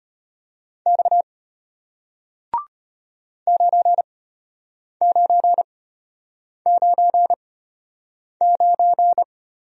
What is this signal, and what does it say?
Morse code, heard as a keyed tone, about 33 wpm, spaced as for 8 wpm, 690 Hz. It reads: X 9999